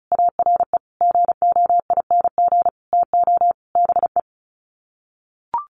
Morse code, heard as a keyed tone, about 34 wpm, 700 Hz. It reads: ARE GOING TO BE
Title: Sets of 5 Words Encore2 35wpm